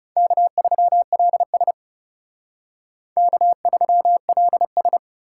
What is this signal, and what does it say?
Morse code, heard as a keyed tone, about 36 wpm, 700 Hz. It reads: K3LS K3LS
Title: Speed Racing - USA 1x2 Call Signs 30wpm